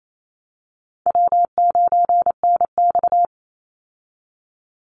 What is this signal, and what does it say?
Morse code, heard as a keyed tone, about 28 wpm, 700 Hz. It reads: W9NX
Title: Easy Call Signs Repeated 2x - 28wpm